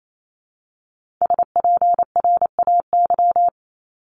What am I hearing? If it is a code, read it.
Morse code, heard as a keyed tone, about 28 wpm, 700 Hz. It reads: SPRAY
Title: Multi-letter Phonemes Repeated 3x 28wpm